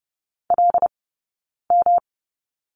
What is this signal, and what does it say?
Morse code, heard as a keyed tone, about 30 wpm, 700 Hz. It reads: L M
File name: Lesson 18d - Letter m - Two Characters 30wpm